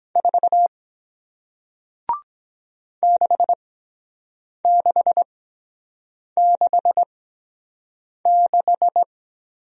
Morse code, heard as a keyed tone, about 23 wpm, spaced as for 7 wpm, 700 Hz. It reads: 4 6666